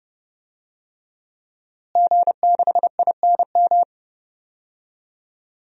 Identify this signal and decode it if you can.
Morse code, heard as a keyed tone, about 30 wpm, 700 Hz. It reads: G6INM
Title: DXCC-100 Call Signs - 5 Characters - Repeated 2x - 30wpm